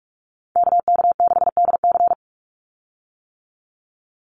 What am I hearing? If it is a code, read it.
Morse code, heard as a keyed tone, about 44 wpm, 700 Hz. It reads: KK6DC
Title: USA Call Signs 45wpm